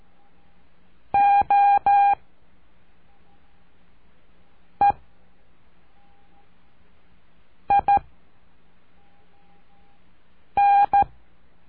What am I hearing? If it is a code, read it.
Morse code, heard as a keyed tone, about 13 wpm, 810 Hz. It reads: O E I N
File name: cursos / CURSO DE TELEGRAFIA / Telegrafia - Trilha 04